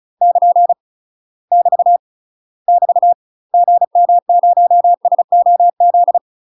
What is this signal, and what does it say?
Morse code, heard as a keyed tone, about 34 wpm, 700 Hz. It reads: <KN> <BT> <BT> GM0SOZ
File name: UK MASTER QSOs TRAINING / 35WPM / UK MASTER QSOs 0003